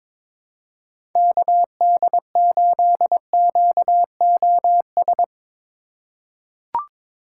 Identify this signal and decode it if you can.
Morse code, heard as a keyed tone, about 22 wpm, 700 Hz. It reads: KD8QOS
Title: USA Call Signs - 6 Characters - 3x Character Spacing - 22wpm